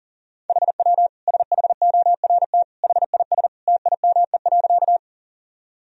Morse code, heard as a keyed tone, about 40 wpm, 700 Hz. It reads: HW SHORT HIS TIME.
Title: Sentences from Top 400 Words and Abbreviations 40wpm